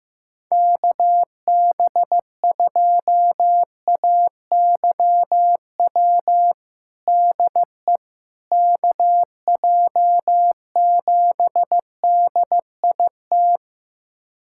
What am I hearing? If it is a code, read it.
Morse code, heard as a keyed tone, about 15 wpm, 700 Hz. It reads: KB2AYW DE KJ7DIT
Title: USA Callsign DE Callsign - 6 Characters - 15wpm